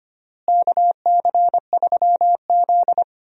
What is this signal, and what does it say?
Morse code, heard as a keyed tone, about 25 wpm, 700 Hz. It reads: KC3Z
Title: USA Call Signs - 4 Characters - Repeated 3x - 25wpm